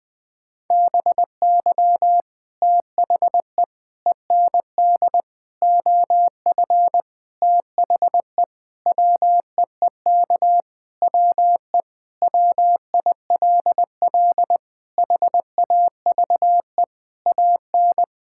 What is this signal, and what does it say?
Morse code, heard as a keyed tone, about 20 wpm, 700 Hz. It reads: BY THE END OF THE WEEK WE WILL HAVE AN
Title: Sentences with 5-Word Phrases - Spoken Beginning - 20wpm